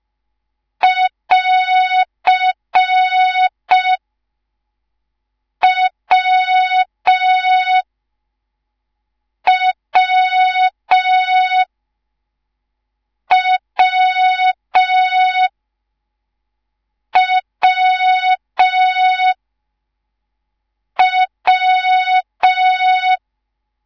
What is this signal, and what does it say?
Morse code, heard as a keyed tone, about 5 wpm, 3700 Hz. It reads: <AR> W W W W W
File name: Morze-HA2MN / hang / 10GWZ HA2MN